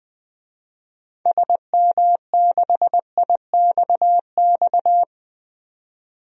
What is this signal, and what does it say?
Morse code, heard as a keyed tone, about 21 wpm, 700 Hz. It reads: SM6IXX
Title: DXCC-100 Call Signs - 6 Characters - 20wpm